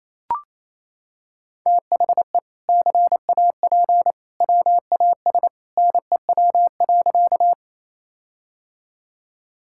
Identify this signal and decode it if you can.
Morse code, heard as a keyed tone, about 28 wpm, 700 Hz. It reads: THE CAP WAS NEW.
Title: Sentences from Top 1600 Words 28wpm